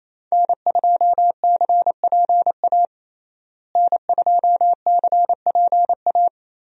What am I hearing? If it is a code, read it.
Morse code, heard as a keyed tone, about 28 wpm, 700 Hz. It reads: N2CPA N2CPA
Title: USA Call Signs - 5 Characters - Repeated 3x - 28wpm